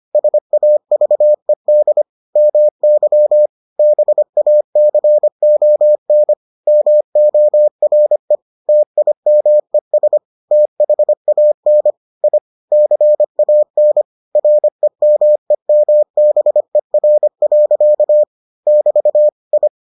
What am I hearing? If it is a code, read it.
Morse code, heard as a keyed tone, about 25 wpm, 600 Hz. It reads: SAVED MY BACON MORE TIMES THAN I CAN REMEMBER. <BT> I